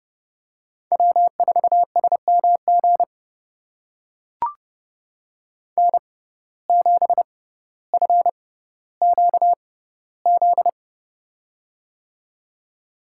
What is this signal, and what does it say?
Morse code, heard as a keyed tone, about 32 wpm, spaced as for 16 wpm, 700 Hz. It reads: W4SMG N7FQZ
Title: USA Call Signs - 5 Characters - 6x Character Spacing - 30wpm